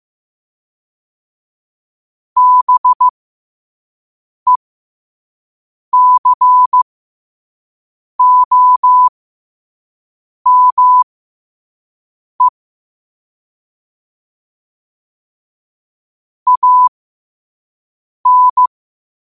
Morse code, heard as a keyed tone, about 15 wpm, spaced as for 5 wpm, 1000 Hz. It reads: BECOME AN
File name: qotdfiles / 5wpm